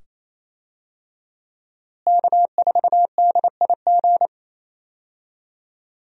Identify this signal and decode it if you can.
Morse code, heard as a keyed tone, about 28 wpm, 700 Hz. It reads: K4DIG
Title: USA Call Signs - 5 Characters - Repeated 2x - 28wpm